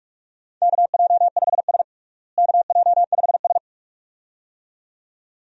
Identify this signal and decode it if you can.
Morse code, heard as a keyed tone, about 46 wpm, 700 Hz. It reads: KJ5S KJ5S
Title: USA Call Signs - 4 Characters - Repeated 2x - 45wpm